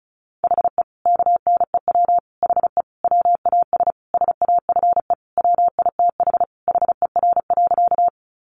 Morse code, heard as a keyed tone, about 34 wpm, 700 Hz. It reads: HE KNEW HE WAS SAFE WITH HER.
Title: Sentences from Top 900 Words 35wpm